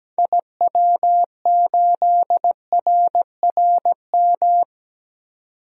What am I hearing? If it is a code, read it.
Morse code, heard as a keyed tone, about 17 wpm, 700 Hz. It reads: IW8RRM
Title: DXCC-100 Call Signs - 6 Characters - 17wpm